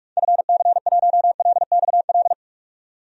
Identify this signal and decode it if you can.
Morse code, heard as a keyed tone, about 45 wpm, 700 Hz. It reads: FC1LXL
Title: DXCC-100 Call Signs - 6 Characters - 45wpm